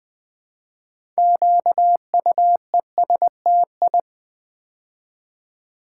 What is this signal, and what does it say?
Morse code, heard as a keyed tone, about 20 wpm, 700 Hz. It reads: QUESTI
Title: Build A Word - 3 Letters At A Time - 8-Letter Words - 20wpm